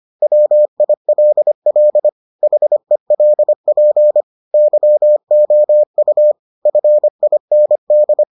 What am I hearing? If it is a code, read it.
Morse code, heard as a keyed tone, about 25 wpm, 600 Hz. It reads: WILL HELP YOU FIND